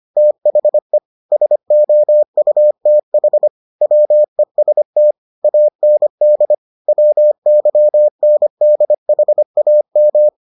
Morse code, heard as a keyed tone, about 25 wpm, 600 Hz. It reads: THE SOUTH WEST AND WYNDHAM